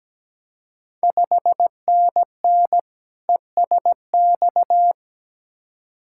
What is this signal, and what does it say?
Morse code, heard as a keyed tone, about 17 wpm, 700 Hz. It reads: 5NN ESX
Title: Ontario QSO Party Exchange 17wpm